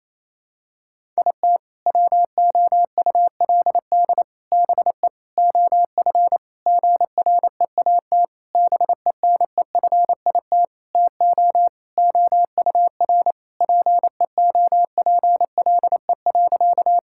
Morse code, heard as a keyed tone, about 28 wpm, 700 Hz. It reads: IT WOULD BE OF GREAT BENEFIT TO OUR PEOPLE.
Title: Sentences from Top 1000 Words - Review Part 6 - 28wpm